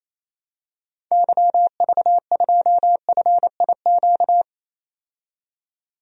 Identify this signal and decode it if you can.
Morse code, heard as a keyed tone, about 27 wpm, 700 Hz. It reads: YV2FIQ
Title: DXCC-100 Call Signs - 6 Characters - 28wpm